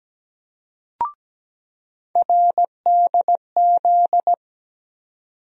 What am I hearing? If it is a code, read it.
Morse code, heard as a keyed tone, about 17 wpm, 700 Hz. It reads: RDZ
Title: USA Call Sign Postfixes - 3 Letters 17wpm